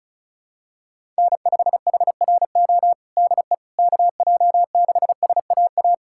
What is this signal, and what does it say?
Morse code, heard as a keyed tone, about 34 wpm, 700 Hz. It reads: N5HRO DE KJ6SAA
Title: USA Callsign DE Callsign 35wpm